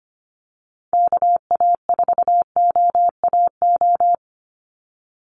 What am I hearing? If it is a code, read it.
Morse code, heard as a keyed tone, about 25 wpm, 700 Hz. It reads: KA4OAO
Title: USA Call Signs 25wpm